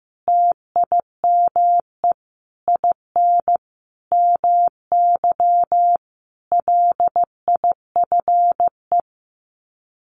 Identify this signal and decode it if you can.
Morse code, heard as a keyed tone, about 15 wpm, 700 Hz. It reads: TIME IN MY LIFE
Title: Sets of 5 Words Encore2 15wpm